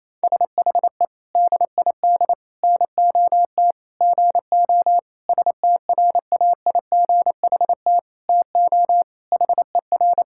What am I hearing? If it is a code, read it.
Morse code, heard as a keyed tone, about 28 wpm, 700 Hz. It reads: SHE DID NOT GO STRAIGHT TO HER
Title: Sentences from Top 1000 Words - Review Part 6 - 28wpm